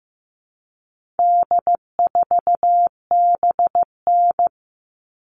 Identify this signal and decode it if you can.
Morse code, heard as a keyed tone, about 15 wpm, 700 Hz. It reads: D4BN
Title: All Cut Numbers - 4-Digit 15wpm